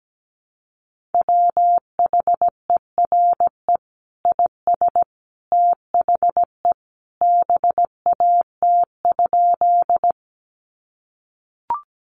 Morse code, heard as a keyed tone, about 17 wpm, 700 Hz. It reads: WHERE IS THE BAT?
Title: Sentences from Top 1500 Words 17wpm